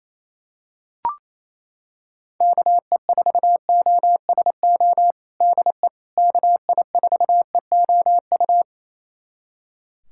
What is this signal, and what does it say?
Morse code, heard as a keyed tone, about 28 wpm, 700 Hz. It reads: KE4OSO DE KI4EOU
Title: USA Callsign DE Callsign - 6 Characters - 28wpm